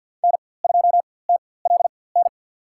Morse code, heard as a keyed tone, about 48 wpm, spaced as for 23 wpm, 700 Hz. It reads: N2TLN
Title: USA Call Signs - 5 Characters - 4x Character Spacing - 50wpm